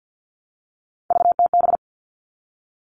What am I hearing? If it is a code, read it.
Morse code, heard as a keyed tone, about 48 wpm, 700 Hz. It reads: VTB
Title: All Cut Numbers - 3-Digit 50wpm